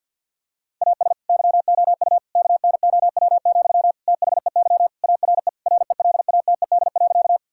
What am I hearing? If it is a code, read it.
Morse code, heard as a keyed tone, about 50 wpm, 700 Hz. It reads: AS YOU KNOW, THEY ARE RELATED.
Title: Sentences from Top 1000 Words - Review Part 5 - 50wpm